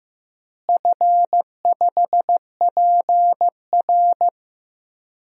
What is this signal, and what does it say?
Morse code, heard as a keyed tone, about 15 wpm, 700 Hz. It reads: F5PR
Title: DXCC-100 Call Signs - 4 Characters - Repeated 3x - 15wpm